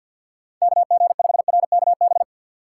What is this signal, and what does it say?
Morse code, heard as a keyed tone, about 51 wpm, 700 Hz. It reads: KG5RKB
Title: USA Call Signs - 6 Characters - 6x Character Spacing - 50wpm